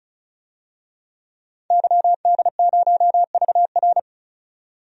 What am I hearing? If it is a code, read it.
Morse code, heard as a keyed tone, about 34 wpm, 700 Hz. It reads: YD0VR
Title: DXCC-100 Call Signs - 5 Characters - 35wpm